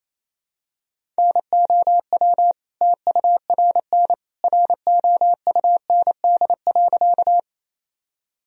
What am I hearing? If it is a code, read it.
Morse code, heard as a keyed tone, about 28 wpm, 700 Hz. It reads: NOW TURN ROUND.
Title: Sentences from Top 1000 Words - Review Part 1 - 28wpm